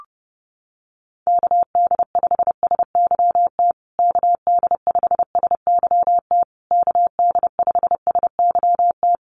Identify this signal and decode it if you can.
Morse code, heard as a keyed tone, about 30 wpm, 700 Hz. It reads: KD5SYT KD5SYT KD5SYT
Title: USA Call Signs Repeated 3x 30wpm